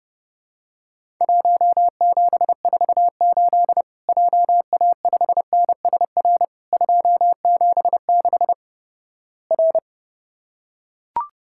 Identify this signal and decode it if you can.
Morse code, heard as a keyed tone, about 30 wpm, 700 Hz. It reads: 1748 JA5NSR 276 R
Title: Worked All Europe DX Contest - QTC Exchange - 30wpm